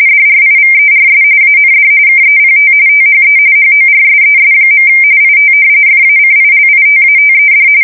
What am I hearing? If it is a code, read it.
RTTY, about 45.45 baud. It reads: DRIVERS AND KEPT THE HORSES. IT FACILITATED
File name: fsk170 / battleshipnj